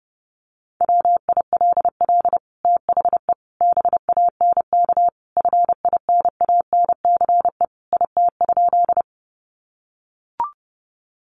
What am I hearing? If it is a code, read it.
Morse code, heard as a keyed tone, about 30 wpm, 700 Hz. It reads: WILL THE BANK FINANCE IT?
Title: Sentences from Top 1500 Words 30wpm